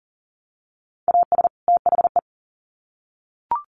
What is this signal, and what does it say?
Morse code, heard as a keyed tone, about 40 wpm, 700 Hz. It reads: AS THE
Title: Sets of 2 Words 40wpm